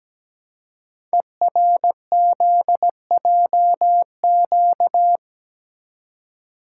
Morse code, heard as a keyed tone, about 17 wpm, 700 Hz. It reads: ERZJQ
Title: Code Groups - 5 Letters - 17wpm